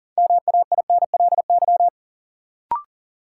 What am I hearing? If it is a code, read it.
Morse code, heard as a keyed tone, about 40 wpm, 700 Hz. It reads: MAINLY
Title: Sentences from Top 1400 Words - New Word Drill - 40wpm